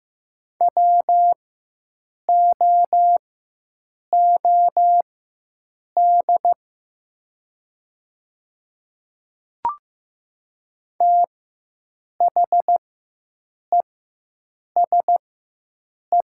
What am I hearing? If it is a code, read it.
Morse code, heard as a keyed tone, about 15 wpm, spaced as for 7 wpm, 700 Hz. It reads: WOOD THESE
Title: 4x Character Spacing - Top 300 Words - 15wpm